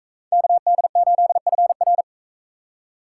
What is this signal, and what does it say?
Morse code, heard as a keyed tone, about 39 wpm, 700 Hz. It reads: KD8FR